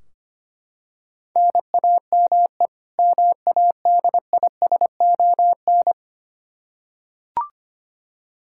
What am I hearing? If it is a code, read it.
Morse code, heard as a keyed tone, about 25 wpm, 700 Hz. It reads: NAME MADISON
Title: Practice QSO - Exchange 1 - Beginning - 25wpm